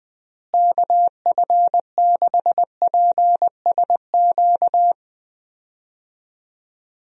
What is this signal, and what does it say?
Morse code, heard as a keyed tone, about 20 wpm, 700 Hz. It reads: KF6PSQ
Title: USA Call Signs - 6 Characters - Repeated 2x - 20wpm